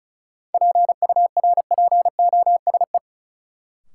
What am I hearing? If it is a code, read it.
Morse code, heard as a keyed tone, about 36 wpm, 700 Hz. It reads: PURPOSE
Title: Sentences from Top 800 Words - New Word Drill - 35wpm